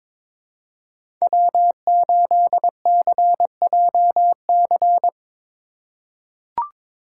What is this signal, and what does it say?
Morse code, heard as a keyed tone, about 23 wpm, 700 Hz. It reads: W8CJC
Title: USA Call Signs - 5 Characters - 2x Character Spacing - 22wpm